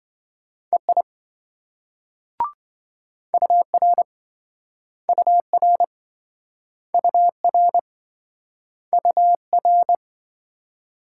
Morse code, heard as a keyed tone, about 27 wpm, 700 Hz. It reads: EI UR UR UR UR